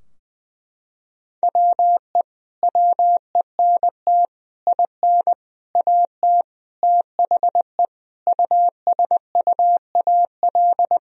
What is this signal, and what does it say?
Morse code, heard as a keyed tone, about 21 wpm, 700 Hz. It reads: WE WENT IN AT THE USUAL
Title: Sentences from Top 1000 Words - Review Part 2 - 20wpm